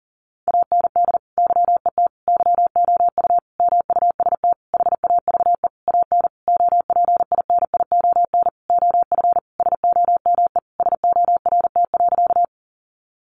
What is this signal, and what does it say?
Morse code, heard as a keyed tone, about 40 wpm, 700 Hz. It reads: AND YET YOU MUST HAVE AN OPINION OF SOME SORT.
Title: Sentences from Top 1300 Words 40wpm